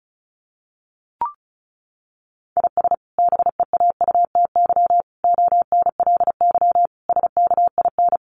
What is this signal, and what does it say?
Morse code, heard as a keyed tone, about 34 wpm, 700 Hz. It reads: IS BEAUTY ONLY SKIN
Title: Sentences from Top 1300 Words 35wpm